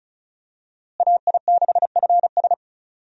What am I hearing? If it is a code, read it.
Morse code, heard as a keyed tone, about 36 wpm, 700 Hz. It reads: AI6FS
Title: USA Call Signs - 5 Characters - Repeated 3x - 35wpm